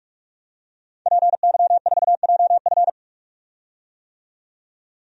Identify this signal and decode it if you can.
Morse code, heard as a keyed tone, about 44 wpm, 700 Hz. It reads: PY4JF
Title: DXCC-100 Call Signs - 5 Characters - 45wpm